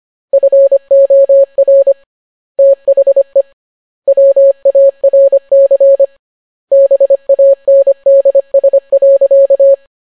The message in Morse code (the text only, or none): FOR THE WARC BANDS.